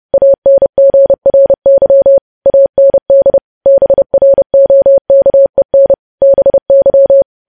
Morse code, heard as a keyed tone, about 30 wpm, 550 Hz. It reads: ANGRY AND BROKEN BY
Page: news headlines in morse code at 30 wpm